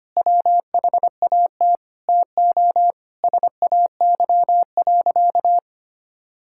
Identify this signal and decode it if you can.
Morse code, heard as a keyed tone, about 25 wpm, 700 Hz. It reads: WHAT TO SAY.
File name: practice / 25wpm / Sentences with 4-Word Phrases - 25wpm